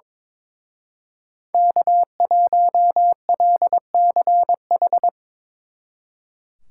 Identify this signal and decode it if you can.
Morse code, heard as a keyed tone, about 24 wpm, 700 Hz. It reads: K1LCH
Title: Speed Racing - USA 1x3 Call Signs 22wpm